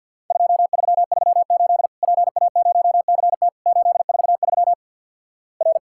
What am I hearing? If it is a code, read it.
Morse code, heard as a keyed tone, about 50 wpm, 700 Hz. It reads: 2338 PA0CT 843 R
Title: Worked All Europe DX Contest - QTC Exchange with Repeats - 50wpm